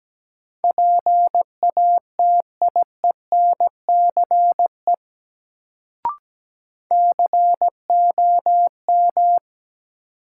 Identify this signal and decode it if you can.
Morse code, heard as a keyed tone, about 17 wpm, 700 Hz. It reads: PATIENCE COM
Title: Sentences from Top 1300 Words - New Word Drill - 17wpm